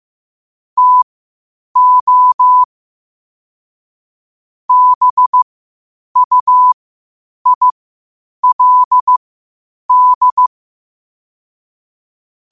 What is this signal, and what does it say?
Morse code, heard as a keyed tone, about 15 wpm, spaced as for 8 wpm, 1000 Hz. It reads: TO BUILD
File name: qotdfiles / 8wpm